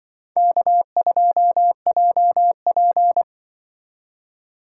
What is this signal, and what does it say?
Morse code, heard as a keyed tone, about 25 wpm, 700 Hz. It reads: K2JP
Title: Speed Racing - USA 1x2 Call Signs 20wpm